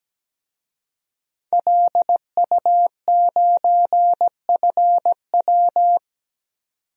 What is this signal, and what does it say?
Morse code, heard as a keyed tone, about 17 wpm, 700 Hz. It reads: LU9FW
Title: DXCC-100 Call Signs - 5 Characters - 17wpm